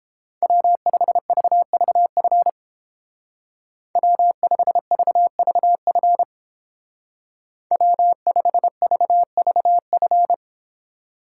Morse code, heard as a keyed tone, about 28 wpm, 700 Hz. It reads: W5VVF W5VVF W5VVF